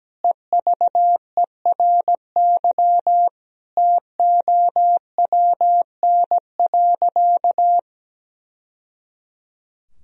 Morse code, heard as a keyed tone, about 17 wpm, 700 Hz. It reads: EVERY TOWN.
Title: Sentences from Top 1000 Words - Review Part 4 - 17wpm